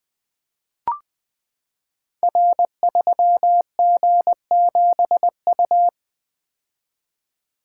Morse code, heard as a keyed tone, about 20 wpm, 700 Hz. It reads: R3G7U
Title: Code Groups - 5 Letter:Numbers - 20wpm